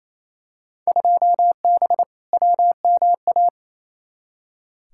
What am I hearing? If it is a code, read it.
Morse code, heard as a keyed tone, about 28 wpm, 700 Hz. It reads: 2B WMA
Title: Field Day Exchanges Encore - 28wpm